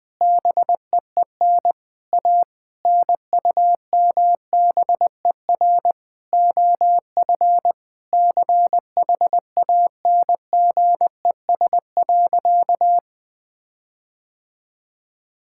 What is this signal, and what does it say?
Morse code, heard as a keyed tone, about 20 wpm, 700 Hz. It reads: BEEN A NUMBER OF CHANGES.
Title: Sentences with 6-Word Phrases - Spoken Beginning - 20wpm